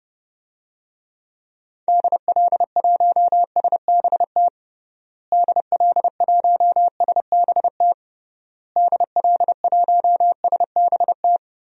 Morse code, heard as a keyed tone, about 31 wpm, 700 Hz. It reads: DL1SBT DL1SBT DL1SBT
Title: DXCC-100 Call Signs - 6 Characters - Repeated 3x - 30wpm